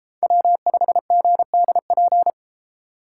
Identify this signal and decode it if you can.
Morse code, heard as a keyed tone, about 30 wpm, 700 Hz. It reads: W5GDP